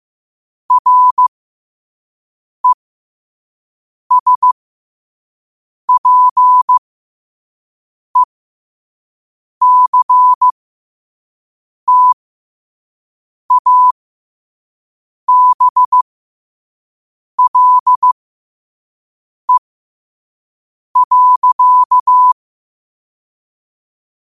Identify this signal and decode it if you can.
Morse code, heard as a keyed tone, about 15 wpm, spaced as for 5 wpm, 1000 Hz. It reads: RESPECTABLE.
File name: qotdfiles / 5wpm